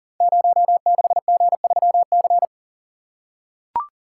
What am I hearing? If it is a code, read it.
Morse code, heard as a keyed tone, about 40 wpm, 690 Hz. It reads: 06G3C
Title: Code Groups - 5 Letter:Numbers - 40wpm